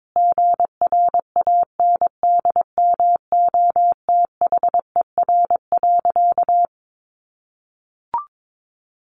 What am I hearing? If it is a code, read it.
Morse code, heard as a keyed tone, about 22 wpm, 700 Hz. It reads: GRANDMOTHER.
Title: Sentences from Top 1200 Words 22wpm